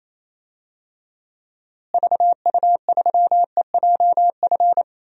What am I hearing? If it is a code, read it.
Morse code, heard as a keyed tone, about 28 wpm, 700 Hz. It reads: VU3EJF
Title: DXCC-100 Call Signs - 6 Characters - Repeated 2x - 28wpm